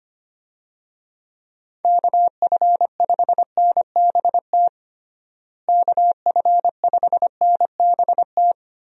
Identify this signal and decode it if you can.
Morse code, heard as a keyed tone, about 25 wpm, 700 Hz. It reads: KF5NBT KF5NBT
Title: USA Call Signs - 6 Characters - Repeated 3x - 25wpm